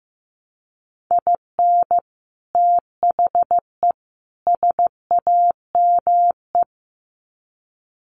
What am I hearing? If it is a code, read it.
Morse code, heard as a keyed tone, about 15 wpm, 700 Hz. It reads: IN THE SAME
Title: Sets of 3 Words 15wpm